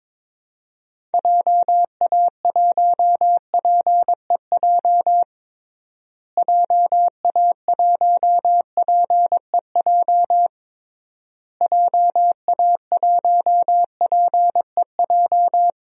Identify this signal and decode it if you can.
Morse code, heard as a keyed tone, about 22 wpm, 700 Hz. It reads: JA1PEJ JA1PEJ JA1PEJ
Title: DXCC-100 Call Signs - 6 Characters - Repeated 3x - 22wpm